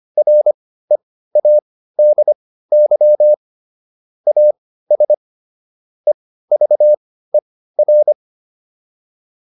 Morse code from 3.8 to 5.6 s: AS

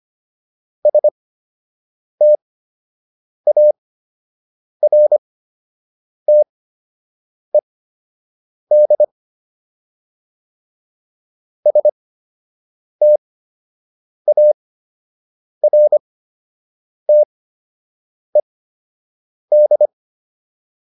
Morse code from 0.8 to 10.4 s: STARTED

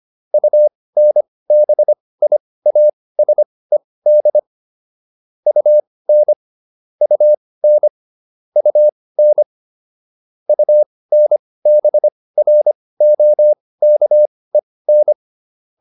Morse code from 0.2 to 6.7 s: UNBIASED UN

